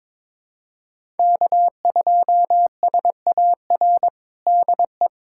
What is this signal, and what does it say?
Morse code, heard as a keyed tone, about 22 wpm, 700 Hz. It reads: K2SAR DE
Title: USA Callsign DE Callsign - 5 Characters - 22wpm